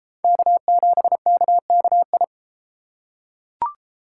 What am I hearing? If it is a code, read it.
Morse code, heard as a keyed tone, about 32 wpm, 700 Hz. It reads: K7KKI